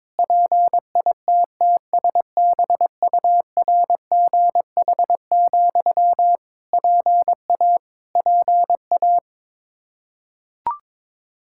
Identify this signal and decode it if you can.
Morse code, heard as a keyed tone, about 22 wpm, 700 Hz. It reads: PITTSBURGH, PA PA
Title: Practice QSO - Exchange 1 - Beginning - 22wpm